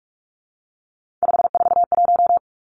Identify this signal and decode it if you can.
Morse code, heard as a keyed tone, about 44 wpm, 700 Hz. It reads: 541
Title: Numbers - 3-Digit 45wpm